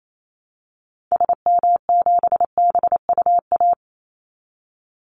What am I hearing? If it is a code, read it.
Morse code, heard as a keyed tone, about 28 wpm, 700 Hz. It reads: SM7BUA
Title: Call Signs Repeated 2x - 28wpm